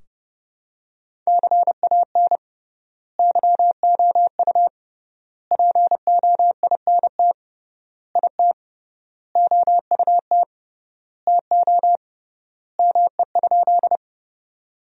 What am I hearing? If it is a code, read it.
Morse code, heard as a keyed tone, about 31 wpm, 700 Hz. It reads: CAN YOU POINT IT OUT TO ME?
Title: 3x Word-Spacing - Sentences from Top 200 Words - 30wpm